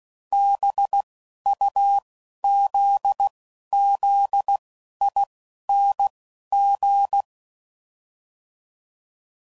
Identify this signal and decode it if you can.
Morse code, heard as a keyed tone, about 16 wpm, 800 Hz. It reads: B U Z Z I N G